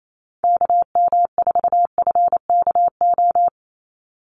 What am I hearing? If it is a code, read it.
Morse code, heard as a keyed tone, about 28 wpm, 700 Hz. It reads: KM4FKO
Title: USA Call Signs 28wpm